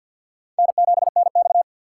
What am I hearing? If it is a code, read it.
Morse code, heard as a keyed tone, about 53 wpm, 700 Hz. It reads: N7NX
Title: USA Call Signs - 4x Character Spacing - 50wpm